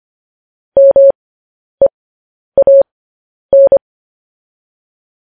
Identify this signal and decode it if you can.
Morse code, heard as a keyed tone, about 25 wpm, spaced as for 10 wpm, 550 Hz. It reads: MEAN
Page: News Headlines in Morse Code at 10 WPM